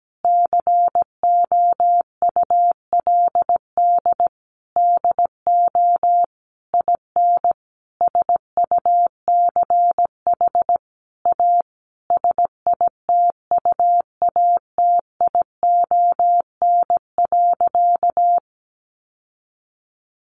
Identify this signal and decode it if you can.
Morse code, heard as a keyed tone, about 17 wpm, 700 Hz. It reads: COULD DO IN SUCH A SITUATION.
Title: Sentences from Top 1200 Words 17wpm